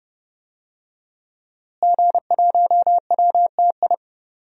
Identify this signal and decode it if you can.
Morse code, heard as a keyed tone, about 30 wpm, 700 Hz. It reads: G1WTI
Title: DXCC-100 Call Signs - 5 Characters - Repeated 2x - 30wpm